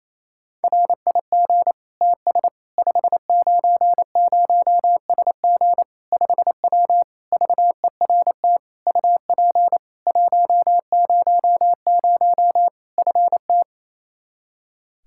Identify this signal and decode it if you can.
Morse code, heard as a keyed tone, about 28 wpm, 700 Hz. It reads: RIG TS 590SG 5W VERT UP 100 FT
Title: Practice QSO - Exchange 2 - Info - 28wpm